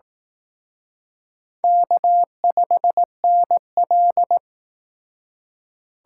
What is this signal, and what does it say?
Morse code, heard as a keyed tone, about 18 wpm, 700 Hz. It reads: K5NL
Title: Speed Racing - USA 1x2 Call Signs 15wpm